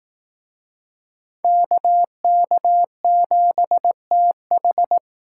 Morse code, heard as a keyed tone, about 18 wpm, 700 Hz. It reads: KK7TH
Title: Speed Racing - USA 2x2 Call Signs 15wpm